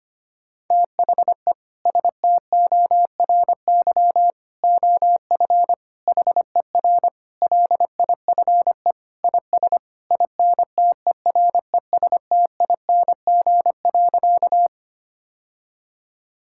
Morse code, heard as a keyed tone, about 25 wpm, 700 Hz. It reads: THE STORY OF HER LIFE IS INTERESTING.
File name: practice / 25wpm / Sentences with 3-Word Phrases - 25wpm